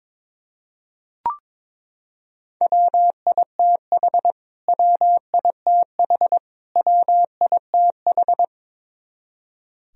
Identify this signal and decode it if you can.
Morse code, heard as a keyed tone, about 22 wpm, 700 Hz. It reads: WITH WITH WITH
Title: Top 100 Words Repeated 22wpm